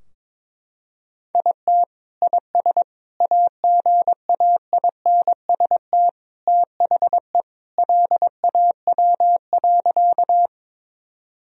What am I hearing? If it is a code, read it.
Morse code, heard as a keyed tone, about 22 wpm, 700 Hz. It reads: IT IS AGAINST THE LAW.
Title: Sentences from Top 1000 Words - Review Part 3 - 22wpm